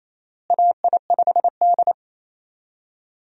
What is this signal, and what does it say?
Morse code, heard as a keyed tone, about 28 wpm, 700 Hz. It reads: AI5D
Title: USA Call Signs - 4 Characters - Repeated 3x - 28wpm